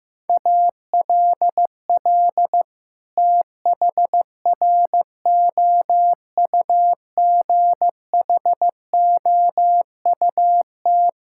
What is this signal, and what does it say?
Morse code, heard as a keyed tone, about 15 wpm, 700 Hz. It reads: ALL THROUGHOUT
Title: Sentences from Top 1000 Words - Review Part 5 - 15wpm